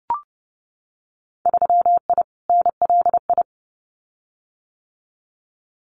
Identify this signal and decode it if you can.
Morse code, heard as a keyed tone, about 30 wpm, 700 Hz. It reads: 3I NLI
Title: Winter Field Day Exchanges Encore 30wpm